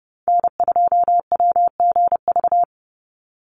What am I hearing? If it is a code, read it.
Morse code, heard as a keyed tone, about 30 wpm, 700 Hz. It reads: N2WGV
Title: USA Call Signs 30wpm